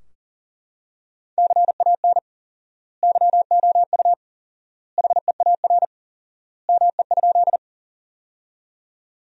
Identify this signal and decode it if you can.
Morse code, heard as a keyed tone, about 40 wpm, 700 Hz. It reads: CAN YOU HEAR ME?
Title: 4x Word-Spacing - Sentences from Top 300 Words - 40wpm